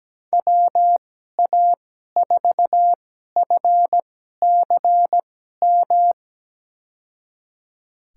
Morse code, heard as a keyed tone, about 17 wpm, spaced as for 12 wpm, 700 Hz. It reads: WA4FCM
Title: USA Call Signs - 6 Characters - 2x Character Spacing - 17wpm